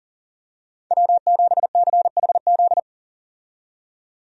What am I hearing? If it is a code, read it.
Morse code, heard as a keyed tone, about 40 wpm, 700 Hz. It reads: W7CHZ
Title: USA Call Signs - 5 Characters - 40wpm